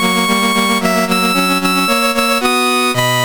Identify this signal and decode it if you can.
There is no code.